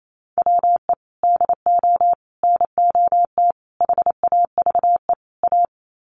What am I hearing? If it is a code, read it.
Morse code, heard as a keyed tone, about 28 wpm, 700 Hz. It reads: WE DO NOT HAVE A
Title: Sets of 5 Words 28wpm